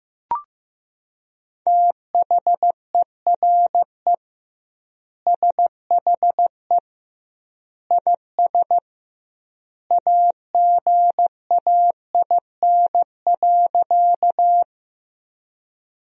Morse code, heard as a keyed tone, about 15 wpm, 700 Hz. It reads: THERE SHE IS AGAIN.
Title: 2x Word-Spacing - Sentences from Top 200 Words - 15wpm